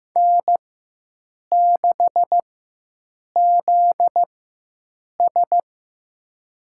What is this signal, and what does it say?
Morse code, heard as a keyed tone, about 15 wpm, spaced as for 7 wpm, 700 Hz. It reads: N6ZS
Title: USA Call Signs - 4x Character Spacing - 15wpm